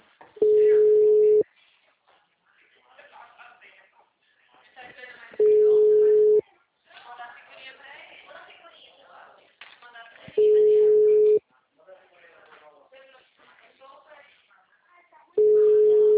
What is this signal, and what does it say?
Ringback tone (Europe, 425 Hz), with speech.